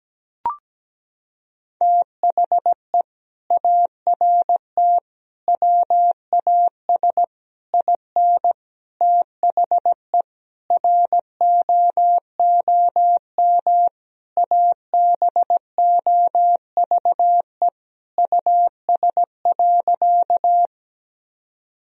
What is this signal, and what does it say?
Morse code, heard as a keyed tone, about 17 wpm, 700 Hz. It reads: THE ART WAS IN THE ROOM ABOVE US.
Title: Sentences from Top 1000 Words - Review Part 3 - 17wpm